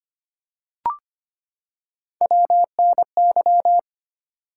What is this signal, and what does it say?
Morse code, heard as a keyed tone, about 25 wpm, 700 Hz. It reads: WNY
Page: morse-code-ninja-practice-25wpm